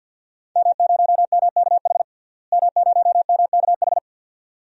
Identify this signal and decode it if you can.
Morse code, heard as a keyed tone, about 47 wpm, 700 Hz. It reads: M0MKH M0MKH
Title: DXCC-100 Call Signs - 5 Characters - Repeated 2x - 50wpm